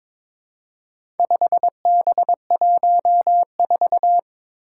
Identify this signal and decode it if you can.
Morse code, heard as a keyed tone, about 22 wpm, 700 Hz. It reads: 5B14
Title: Four Challenging Letter-Number Combinations 22wpm